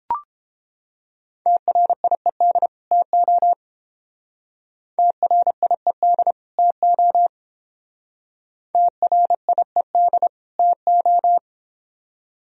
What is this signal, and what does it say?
Morse code, heard as a keyed tone, about 31 wpm, 700 Hz. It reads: TRIED TO TRIED TO TRIED TO